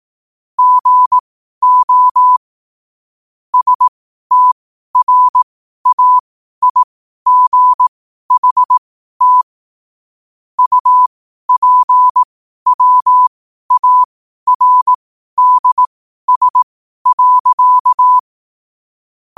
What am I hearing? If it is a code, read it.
Morse code, heard as a keyed tone, about 18 wpm, spaced as for 12 wpm, 1000 Hz. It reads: GO STRAIGHT UPWARDS.
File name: qotdfiles / 12wpm